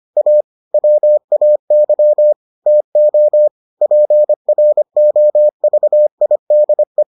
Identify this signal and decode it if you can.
Morse code, heard as a keyed tone, about 25 wpm, 600 Hz. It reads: A WAY TO PROVIDE